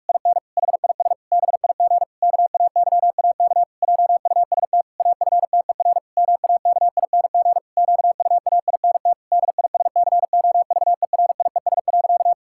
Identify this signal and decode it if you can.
Morse code, heard as a keyed tone, about 46 wpm, 700 Hz. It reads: IN HIS BIG KAYAK JUST AFTER MAKING QUAINT DISCOVERIES.
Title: Pangram Challenge v2 - 45wpm